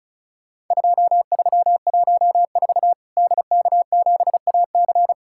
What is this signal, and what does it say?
Morse code, heard as a keyed tone, about 36 wpm, 700 Hz. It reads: 2314 DK7AC
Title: Worked All Europe DX Contest - QTC Exchange - 35wpm